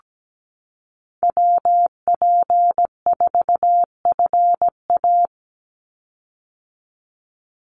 Morse code, heard as a keyed tone, about 17 wpm, 700 Hz. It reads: WP4FA
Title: USA Call Signs 17wpm